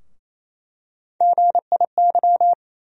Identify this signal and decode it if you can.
Morse code, heard as a keyed tone, about 28 wpm, 700 Hz. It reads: GIY
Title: USA Call Sign Postfixes - 3 Letters 28wpm